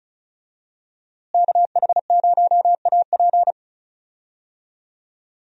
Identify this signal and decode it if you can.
Morse code, heard as a keyed tone, about 34 wpm, 700 Hz. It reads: KH0AP
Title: USA Call Signs - 5 Characters - Repeated 3x - 35wpm